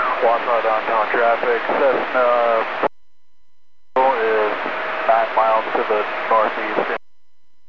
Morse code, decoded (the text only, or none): none